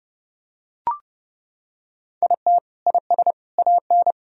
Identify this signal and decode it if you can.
Morse code, heard as a keyed tone, about 30 wpm, 700 Hz. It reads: IT IS AN
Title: Sentences from Top 1000 Words - Review Part 4 - 30wpm